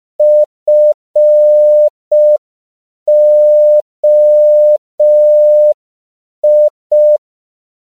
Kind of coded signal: Morse code